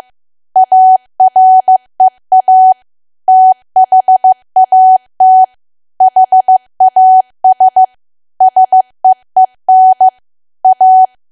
Morse code, heard as a keyed tone, about 15 wpm, 750 Hz. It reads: AREA THAT HAS SEEN A
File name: cwa / W1AW-Level2 / 15WPM